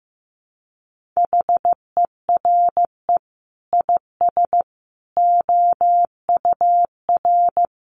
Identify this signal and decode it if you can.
Morse code, heard as a keyed tone, about 15 wpm, 700 Hz. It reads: HERE IS OUR
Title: Sentences from Top 500 Words 15wpm